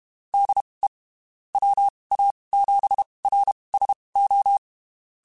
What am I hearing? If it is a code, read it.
Morse code, heard as a keyed tone, about 32 wpm, spaced as for 23 wpm, 800 Hz. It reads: DE WA7RSO
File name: (spaced) / QSOs Spaced-mp3s / kc7haa @30 sp